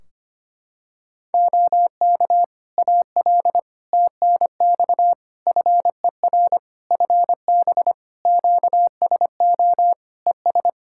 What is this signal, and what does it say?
Morse code, heard as a keyed tone, about 25 wpm, 700 Hz. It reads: OK AL TNX FER FB QSO ES
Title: Practice QSO - Exchange 4 - Ending - 25wpm